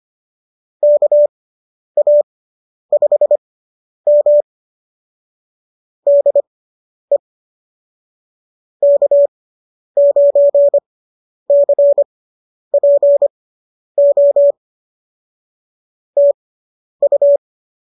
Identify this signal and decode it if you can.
Morse code, heard as a keyed tone, about 25 wpm, spaced as for 10 wpm, 590 Hz. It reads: KA5M DE K9CPO TU